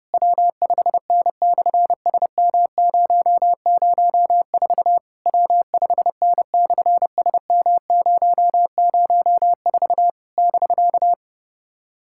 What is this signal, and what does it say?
Morse code, heard as a keyed tone, about 30 wpm, 700 Hz. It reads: W5N/SM004 W5N/SM004 <BK>
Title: SOTA QSO Practice 30wpm